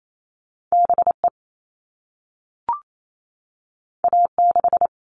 Morse code, heard as a keyed tone, about 28 wpm, 700 Hz. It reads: BE A6
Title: All Cut Numbers - 2-Digit 28wpm